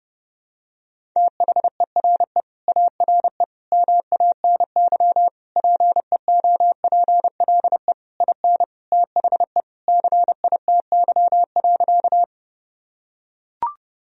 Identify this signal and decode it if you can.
Morse code, heard as a keyed tone, about 30 wpm, 700 Hz. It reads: THERE ARE MANY PEOPLE IN THE CITY.
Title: Sentences from Top 1000 Words - Review Part 1 - 30wpm